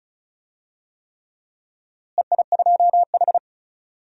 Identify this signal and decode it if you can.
Morse code, heard as a keyed tone, about 36 wpm, 700 Hz. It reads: EI2H
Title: DXCC-100 Call Signs - 4 Characters - Repeated 3x - 35wpm